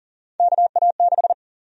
Morse code, heard as a keyed tone, about 40 wpm, 700 Hz. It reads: KA6
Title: USA Call Sign Prefixes - 40wpm